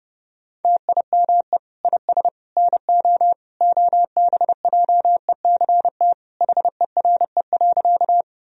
Morse code, heard as a keyed tone, about 30 wpm, 700 Hz. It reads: TIME IS NO OBJECT HERE.
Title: Sentences from Top 1000 Words - Review Part 3 - 30wpm